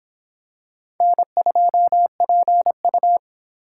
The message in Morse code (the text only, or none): N2PU